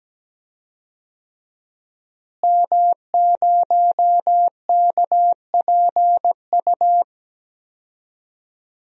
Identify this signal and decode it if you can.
Morse code, heard as a keyed tone, about 17 wpm, 700 Hz. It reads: M0KPU
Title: DXCC-100 Call Signs - 5 Characters - Repeated 3x - 17wpm